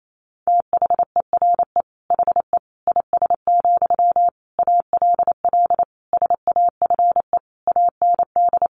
Morse code, heard as a keyed tone, about 28 wpm, 690 Hz. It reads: THERE HE IS, ALL SAFE AND
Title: Sentences from Top 900 Words 28wpm